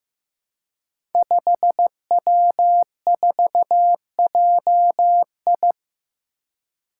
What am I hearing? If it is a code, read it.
Morse code, heard as a keyed tone, about 15 wpm, 700 Hz. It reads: 5W4JI
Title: Five Challenging Letter-Number Combinations 15wpm